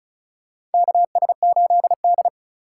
Morse code, heard as a keyed tone, about 36 wpm, 690 Hz. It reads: KS8D
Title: USA Call Signs - 3x Character Spacing - 35wpm